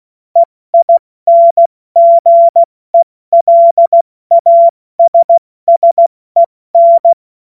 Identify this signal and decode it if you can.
Morse code, heard as a keyed tone, about 16 wpm, 680 Hz. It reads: EINGELASSEN
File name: Morsekurs / Witze / Witz 21. September 2021, 70 F80